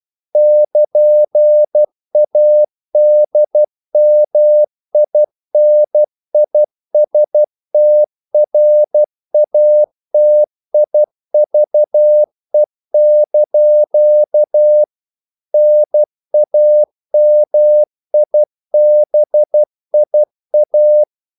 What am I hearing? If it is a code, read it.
Morse code, heard as a keyed tone, about 12 wpm, 600 Hz. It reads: <KN>ADMINISTRATIVE) NAMIBIA
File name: CW / Practice / NationalCapitals12